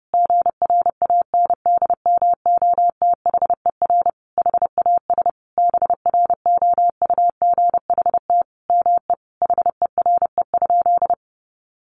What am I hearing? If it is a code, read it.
Morse code, heard as a keyed tone, about 30 wpm, 700 Hz. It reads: GRANDMOTHER HAS BROUGHT ME HERE?
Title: Sentences from Top 1200 Words 30wpm